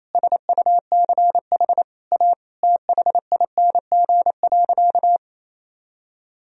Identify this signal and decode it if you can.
Morse code, heard as a keyed tone, about 28 wpm, 700 Hz. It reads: SUCH A THING.
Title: Sentences from Top 200 Words and Abbreviations 28wpm